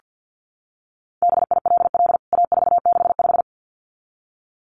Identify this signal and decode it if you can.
Morse code, heard as a keyed tone, about 49 wpm, 700 Hz. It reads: BILL A465
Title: CWT Exchange 50wpm